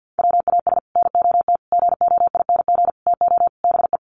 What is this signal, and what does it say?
Morse code, heard as a keyed tone, about 50 wpm, 700 Hz. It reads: WAS NOT GOING TO BE
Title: Sets of 5 Words Encore 50wpm